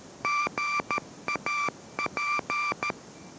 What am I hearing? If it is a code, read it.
Morse code, heard as a keyed tone, about 13 wpm, 1180 Hz, with noise. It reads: GAP